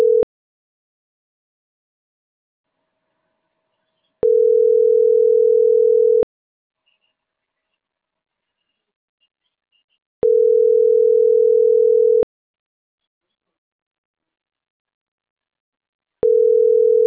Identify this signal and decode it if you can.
Ringback tone (North America), 440 + 480 Hz.